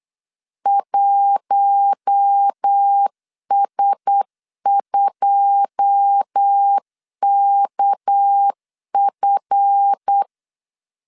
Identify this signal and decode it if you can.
Morse code, heard as a keyed tone, about 8 wpm, 800 Hz. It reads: 1S2KF
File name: LEZIONI / 10 SIGNS / 10D COMA 8wpm 3749